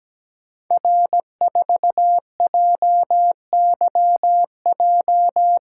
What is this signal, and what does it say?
Morse code, heard as a keyed tone, about 17 wpm, 700 Hz. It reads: R4JYJ
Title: Code Groups - 5 Letter:Numbers - 17wpm